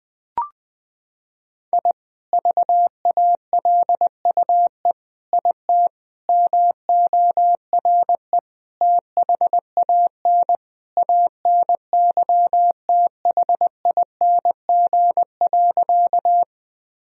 Morse code, heard as a keyed tone, about 21 wpm, 700 Hz. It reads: I VALUE IT MORE THAN ANYTHING.
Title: Sentences from Top 1000 Words - Review Part 3 - 20wpm